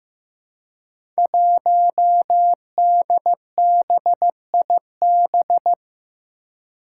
Morse code, heard as a keyed tone, about 15 wpm, 700 Hz. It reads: 1DBIB
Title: Five Challenging Letter-Number Combinations 15wpm